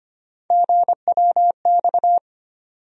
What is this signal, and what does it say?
Morse code, heard as a keyed tone, about 25 wpm, 700 Hz. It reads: GWX
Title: USA Call Sign Postfixes - 3 Letters 25wpm